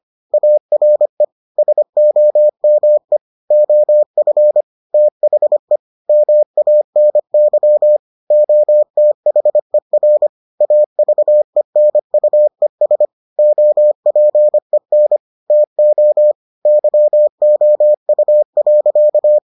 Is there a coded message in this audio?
Yes: Morse code from 0.3 to 19.4 s, heard as a keyed tone, about 25 wpm, 600 Hz. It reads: ARE SOME OF THE MANY OTHER AVENUES OPEN TO YOU.